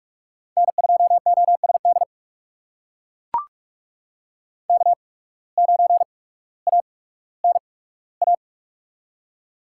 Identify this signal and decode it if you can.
Morse code, heard as a keyed tone, about 44 wpm, spaced as for 19 wpm, 700 Hz. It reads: N2OSD K9ANA
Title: USA Call Signs - 5 Characters - 8x Character Spacing - 45wpm